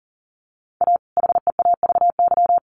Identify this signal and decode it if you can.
Morse code, heard as a keyed tone, about 40 wpm, 700 Hz. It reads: A HEAVY
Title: Sentences from Top 1100 Words 40wpm